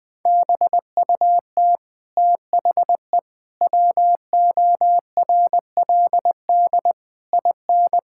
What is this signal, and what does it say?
Morse code, heard as a keyed tone, about 20 wpm, 700 Hz. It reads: BUT THE WORLD IN
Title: Sentences from Top 1000 Words - Review Part 6 - 20wpm